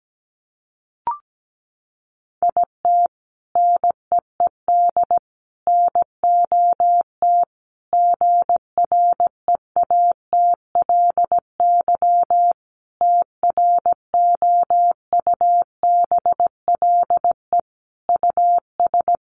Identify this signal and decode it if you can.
Morse code, heard as a keyed tone, about 17 wpm, 700 Hz. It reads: IT NEED NOT GREATLY TROUBLE US
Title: Sentences from Top 1400 Words 17wpm